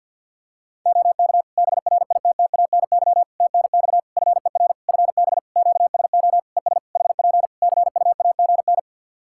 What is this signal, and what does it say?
Morse code, heard as a keyed tone, about 47 wpm, 700 Hz. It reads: OK BRITTANY TNX FER FB QSO ES HP CUAGN
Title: Practice QSO - Exchange 4 - Ending - 50wpm